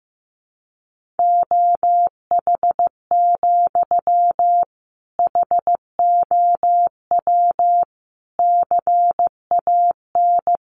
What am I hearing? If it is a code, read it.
Morse code, heard as a keyed tone, about 15 wpm, 700 Hz. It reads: OH, HOW CAN
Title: Sentences from Top 500 Words 15wpm